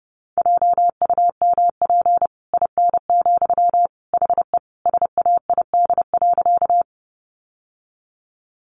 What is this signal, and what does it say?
Morse code, heard as a keyed tone, about 30 wpm, 700 Hz. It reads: JUMP IN, HE SAID.
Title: Sentences from Top 1300 Words 30wpm